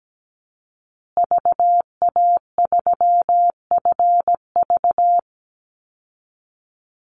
Morse code, heard as a keyed tone, about 17 wpm, 700 Hz. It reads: VA3FV
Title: Call Signs Repeated 2x - 17wpm